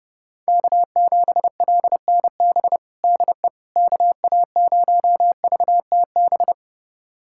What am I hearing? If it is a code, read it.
Morse code, heard as a keyed tone, about 30 wpm, 700 Hz. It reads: K7LNB DE KA0VTB
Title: USA Callsign DE Callsign 30wpm